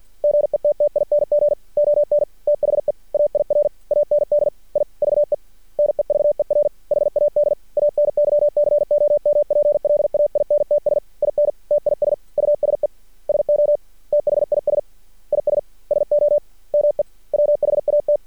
Morse code, heard as a keyed tone, about 50 wpm, 590 Hz. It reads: GETTING ON THE AIR AND IVE NEVER HAD ANY COMPLAINTS IN TIS USE SO THIS IS SOMEWHAT
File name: archived / CRSnet